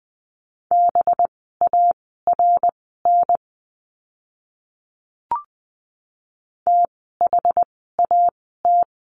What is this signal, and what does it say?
Morse code, heard as a keyed tone, about 20 wpm, spaced as for 14 wpm, 700 Hz. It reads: BARN THAT
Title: 185 Lesson 30f - Letter b - Words 20wpm